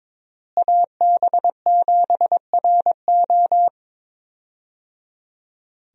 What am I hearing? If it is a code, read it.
Morse code, heard as a keyed tone, about 22 wpm, 700 Hz. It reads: AB7RO
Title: USA Call Signs - 5 Characters - Repeated 3x - 22wpm